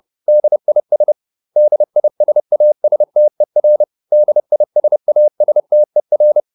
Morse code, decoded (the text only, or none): DIS DISASTER DISASTER